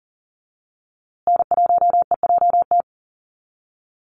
Morse code, heard as a keyed tone, about 40 wpm, 700 Hz. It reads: N1EJT
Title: USA Call Signs Repeated 2x 40wpm